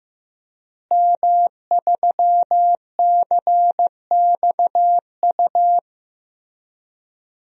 Morse code, heard as a keyed tone, about 15 wpm, 700 Hz. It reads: M3CXU
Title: DXCC-100 Call Signs - 5 Characters - Repeated 3x - 15wpm